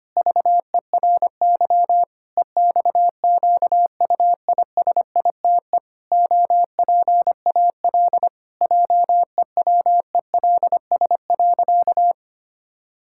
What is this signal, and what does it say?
Morse code, heard as a keyed tone, about 25 wpm, 700 Hz. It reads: VERY EXQUISITE OPAL JEWELS.
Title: Pangram Challenge v2 - 25wpm